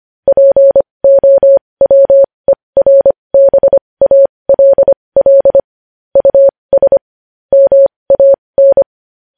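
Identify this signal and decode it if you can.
Morse code, heard as a keyed tone, about 25 wpm, spaced as for 20 wpm, 550 Hz. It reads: POWERBALL US MAN
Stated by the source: Morse news headlines sent at 20.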